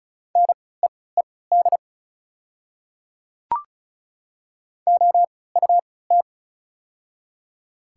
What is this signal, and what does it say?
Morse code, heard as a keyed tone, about 36 wpm, spaced as for 20 wpm, 700 Hz. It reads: NEED OUT
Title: 3x Character Spacing - Top 300 Words - 35wpm